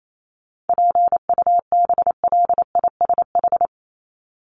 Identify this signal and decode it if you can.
Morse code, heard as a keyed tone, about 28 wpm, 700 Hz. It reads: PUBLISH
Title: ED Suffix - Top 100 Words with Base Words 28wpm